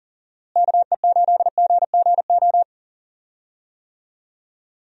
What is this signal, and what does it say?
Morse code, heard as a keyed tone, about 40 wpm, 700 Hz. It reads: KE8GGO
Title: USA Call Signs - 6 Characters - Repeated 2x - 40wpm